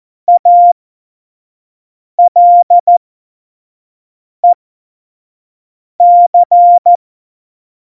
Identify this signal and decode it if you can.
Morse code, heard as a keyed tone, about 14 wpm, spaced as for 5 wpm, 700 Hz. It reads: ALEC
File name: Morsekurs / Witze / Witz 03. November 2020, 25 F70